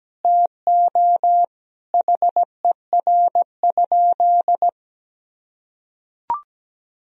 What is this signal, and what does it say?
Morse code, heard as a keyed tone, about 17 wpm, 700 Hz. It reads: TO HER?
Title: Sentences from Top 1000 Words - Review Part 2 - 17wpm